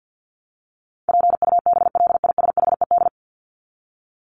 Morse code, heard as a keyed tone, about 50 wpm, 700 Hz. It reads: PUBLISHED
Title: ED Suffix - Top 100 Words with Base Words 50wpm